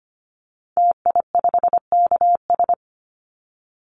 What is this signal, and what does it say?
Morse code, heard as a keyed tone, about 25 wpm, 700 Hz. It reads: TI5KS
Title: Call Signs Repeated 2x - 25wpm